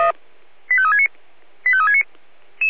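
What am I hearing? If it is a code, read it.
Five-tone selective call (ZVEI), heard as quick runs of five tones, tones of about 70 ms. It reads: 85369 85369
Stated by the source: ZVEI five-tone selective call received on SDR.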